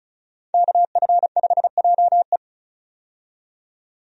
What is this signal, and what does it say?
Morse code, heard as a keyed tone, about 36 wpm, 700 Hz. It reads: KF5JE
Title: USA Call Signs - 5 Characters - 35wpm